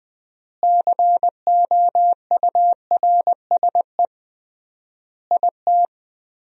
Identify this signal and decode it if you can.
Morse code, heard as a keyed tone, about 20 wpm, 700 Hz. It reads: COURSE IT
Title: 3x Word-Spacing - Sentences from Top 500 Words - 20wpm